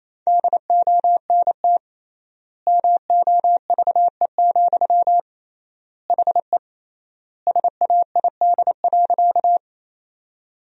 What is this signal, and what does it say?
Morse code, heard as a keyed tone, about 28 wpm, 700 Hz. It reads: DONT MOVE, HE SAID.
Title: 3x Word-Spacing - Sentences from Top 300 Words - 28wpm